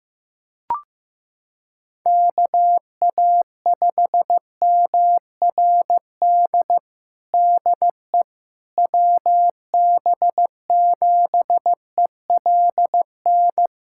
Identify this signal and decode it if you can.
Morse code, heard as a keyed tone, about 15 wpm, 700 Hz. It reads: KA5MRD DE WB7ELN
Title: USA Callsign DE Callsign 15wpm